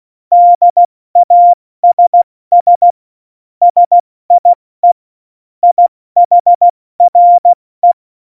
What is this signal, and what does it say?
Morse code, heard as a keyed tone, about 16 wpm, 700 Hz. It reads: DASS SIE IHRE
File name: Morsekurs / Witze / Witz 25. Februar 2020, 70 F80